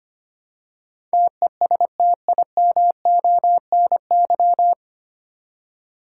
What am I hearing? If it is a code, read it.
Morse code, heard as a keyed tone, about 25 wpm, 700 Hz. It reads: TESTIMONY
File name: Nine Letter Words - 25wpm